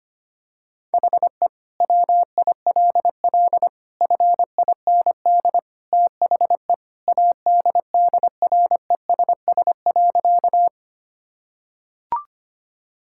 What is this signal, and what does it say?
Morse code, heard as a keyed tone, about 25 wpm, 700 Hz. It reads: HE WILL FIND THE ADDRESS.
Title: Sentences from Top 1000 Words - Review Part 6 - 25wpm